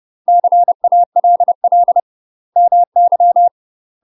Morse code, heard as a keyed tone, about 30 wpm, 700 Hz. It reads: CALL MY
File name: UK MASTER QSOs TRAINING / 30WPM / UK MASTER QSOs 0019